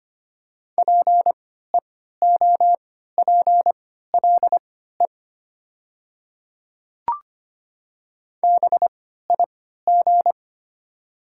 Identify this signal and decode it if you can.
Morse code, heard as a keyed tone, about 25 wpm, spaced as for 14 wpm, 700 Hz. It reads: PEOPLE BIG
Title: 3x Character Spacing - Top 200 Words - 25wpm